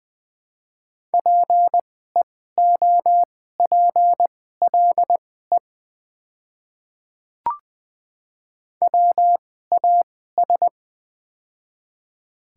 Morse code, heard as a keyed tone, about 21 wpm, spaced as for 15 wpm, 700 Hz. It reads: PEOPLE WAS
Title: 2x Character Spacing - Top 100 Words - 20wpm